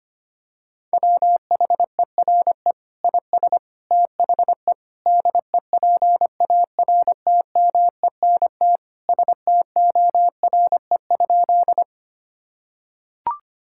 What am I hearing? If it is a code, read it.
Morse code, heard as a keyed tone, about 25 wpm, 700 Hz. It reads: WHERE IS THE DEPARTMENT STORE?
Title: Sentences from Top 1000 Words - Review Part 5 - 25wpm